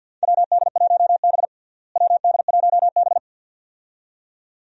Morse code, heard as a keyed tone, about 51 wpm, 700 Hz. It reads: WD1B WD1B
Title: USA Call Signs - 4 Characters - Repeated 2x - 50wpm